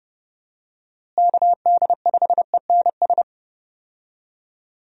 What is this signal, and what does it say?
Morse code, heard as a keyed tone, about 30 wpm, 700 Hz. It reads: KD5ENS
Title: USA Call Signs - 6 Characters - Repeated 2x - 30wpm